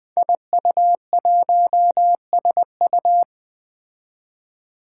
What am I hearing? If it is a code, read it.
Morse code, heard as a keyed tone, about 21 wpm, 700 Hz. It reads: IU1SU
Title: Five Challenging Letter-Number Combinations 20wpm